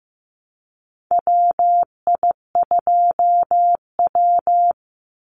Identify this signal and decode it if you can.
Morse code, heard as a keyed tone, about 15 wpm, 700 Hz. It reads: WI2W
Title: NA SOTA Chaser Callsigns 15wpm